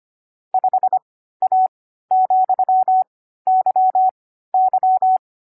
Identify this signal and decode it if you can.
Morse code, heard as a keyed tone, about 25 wpm, spaced as for 14 wpm, 750 Hz. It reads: 5A,YY
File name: CW-Koch / koch.29.3.groups5.15wpm